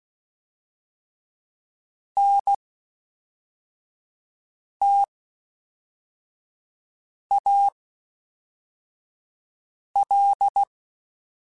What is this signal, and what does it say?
Morse code, heard as a keyed tone, about 16 wpm, spaced as for 4 wpm, 800 Hz. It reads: NTAL
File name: (spaced) / FromSMs spaced mp3s / Classes@5